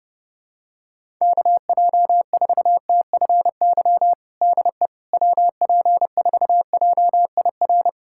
Morse code, heard as a keyed tone, about 30 wpm, 700 Hz. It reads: KJ4TFY DE WP4JIR
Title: USA Callsign DE Callsign 30wpm